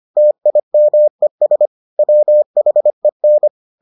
Morse code, heard as a keyed tone, about 25 wpm, 600 Hz. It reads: TIMES WHEN